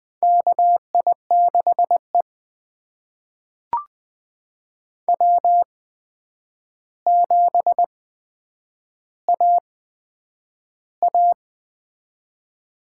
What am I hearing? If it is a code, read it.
Morse code, heard as a keyed tone, about 21 wpm, spaced as for 9 wpm, 700 Hz. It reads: KI6E W7AA
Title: USA Call Signs - 8x Character Spacing - 20wpm